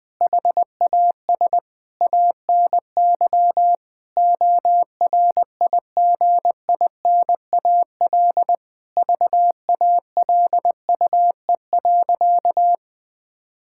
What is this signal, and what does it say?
Morse code, heard as a keyed tone, about 20 wpm, 700 Hz. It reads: HAS ANY ORIGINAL VALUE.
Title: Sentences from Top 1000 Words - Review Part 6 - 20wpm